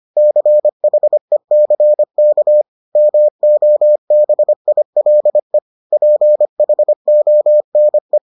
Morse code, heard as a keyed tone, about 25 wpm, 600 Hz. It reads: CHECK MOBILE PHONE